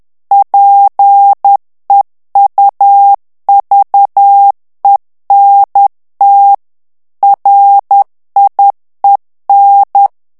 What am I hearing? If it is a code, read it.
Morse code, heard as a keyed tone, about 10 wpm, 800 Hz. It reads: PEUVENT RIEN
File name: morse mp3 / FRANCAIS TEXTE CLAIR 2 VITESSE 10 MOTS MINUTES